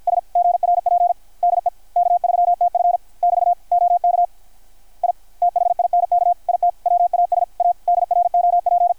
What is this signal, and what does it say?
Morse code, heard as a keyed tone, about 53 wpm, 720 Hz. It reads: S GRP DE K3TF <BT> OK I THINK IT WAS A DROP
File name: archived / CRSnet